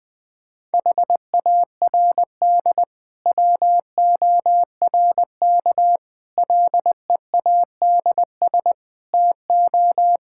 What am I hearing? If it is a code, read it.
Morse code, heard as a keyed tone, about 21 wpm, 700 Hz. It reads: HARD WORK LEADS TO
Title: Difficult Sentences with Hints - 20wpm